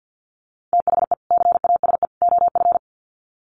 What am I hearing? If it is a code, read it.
Morse code, heard as a keyed tone, about 48 wpm, 700 Hz. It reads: THE CASE OF
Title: Sets of 3 Words 50wpm